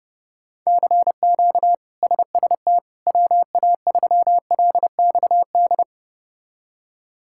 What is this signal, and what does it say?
Morse code, heard as a keyed tone, about 30 wpm, 700 Hz. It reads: CQ SST WA3LXD
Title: K1USN SST QSO Practice - 30wpm